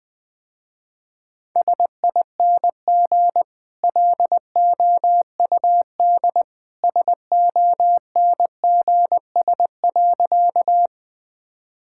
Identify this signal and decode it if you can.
Morse code, heard as a keyed tone, about 20 wpm, 700 Hz. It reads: SING LOUD SONGS.
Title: Easy Sentences with Hints - 20wpm